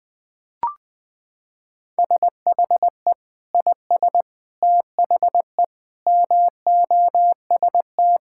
Morse code, heard as a keyed tone, about 20 wpm, 700 Hz. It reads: SHE IS THE MOST
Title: Sentences with 2-Word Phrases - 20wpm